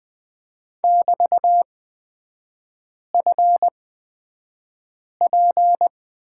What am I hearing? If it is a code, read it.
Morse code, heard as a keyed tone, about 20 wpm, spaced as for 5 wpm, 700 Hz. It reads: <BT>FP
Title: 255 Lesson 41e - Period - Three Characters 20wpm